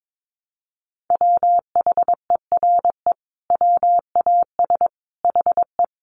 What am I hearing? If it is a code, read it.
Morse code, heard as a keyed tone, about 22 wpm, 700 Hz. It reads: WHERE WAS HE
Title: Sentences from Top 1000 Words 22wpm